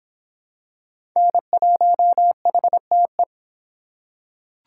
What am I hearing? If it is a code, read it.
Morse code, heard as a keyed tone, about 26 wpm, 700 Hz. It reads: N1HTE